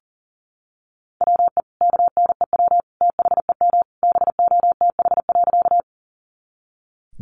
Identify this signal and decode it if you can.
Morse code, heard as a keyed tone, about 40 wpm, 700 Hz. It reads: WE KNEW THEM BOTH.
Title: Sentences from Top 400 Words 40wpm